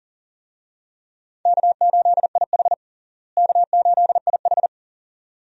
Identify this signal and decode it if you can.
Morse code, heard as a keyed tone, about 40 wpm, 700 Hz. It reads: K8IH K8IH
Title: USA Call Signs - 4 Characters - Repeated 2x - 40wpm